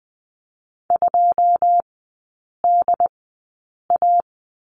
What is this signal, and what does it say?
Morse code, heard as a keyed tone, about 20 wpm, spaced as for 8 wpm, 690 Hz. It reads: 2DA
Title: 081 Lesson 14e - Number 2 - Three Characters 20wpm